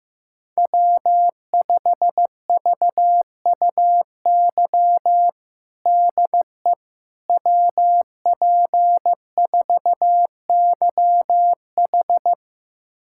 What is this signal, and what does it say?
Morse code, heard as a keyed tone, about 15 wpm, 700 Hz. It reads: W5VUY DE WP4YH
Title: USA Callsign DE Callsign - 5 Characters - 15wpm